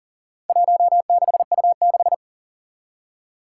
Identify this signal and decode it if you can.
Morse code, heard as a keyed tone, about 40 wpm, 700 Hz. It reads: 16U6
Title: Four Challenging Letter-Number Combinations 40wpm